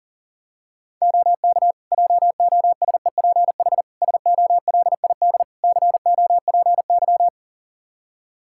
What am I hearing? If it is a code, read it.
Morse code, heard as a keyed tone, about 40 wpm, 700 Hz. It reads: OK JOSEPH SOLID COPY
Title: Practice QSO - Exchange 3 - More Info - 40wpm